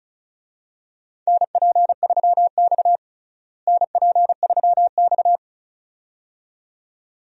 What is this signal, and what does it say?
Morse code, heard as a keyed tone, about 34 wpm, 700 Hz. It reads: NP3X NP3X
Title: USA Call Signs - 4 Characters - Repeated 2x - 35wpm